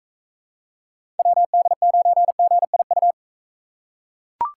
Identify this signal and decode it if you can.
Morse code, heard as a keyed tone, about 41 wpm, 700 Hz. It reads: WD9GIU